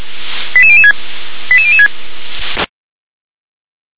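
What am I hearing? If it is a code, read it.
Five-tone selective call (ZVEI), heard as quick runs of five tones, tones of about 70 ms. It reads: 88006 88006
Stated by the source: ZVEI five-tone selective call 88162 received with an SDR.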